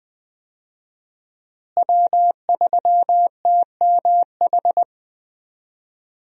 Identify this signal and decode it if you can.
Morse code, heard as a keyed tone, about 20 wpm, 700 Hz. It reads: W3TMH
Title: USA Call Signs - 5 Characters - Repeated 2x - 20wpm